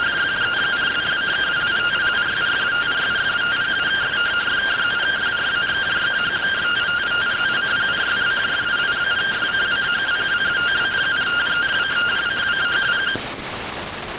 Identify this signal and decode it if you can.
Two-tone teleprinter signal, about 170 Hz shift.